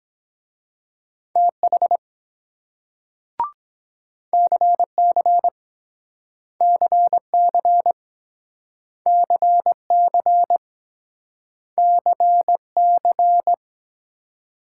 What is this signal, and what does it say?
Morse code, heard as a keyed tone, about 24 wpm, 700 Hz. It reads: TH CC CC CC CC